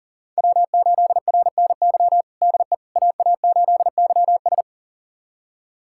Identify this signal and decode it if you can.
Morse code, heard as a keyed tone, about 40 wpm, 700 Hz. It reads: W8RNY DE AA8YS
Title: USA Callsign DE Callsign 40wpm